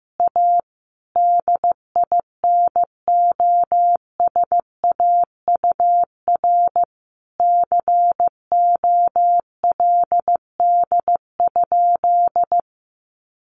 Morse code, heard as a keyed tone, about 14 wpm, 700 Hz. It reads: A DINOSAUR COLD?
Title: A Sound Of Thunder 15wpm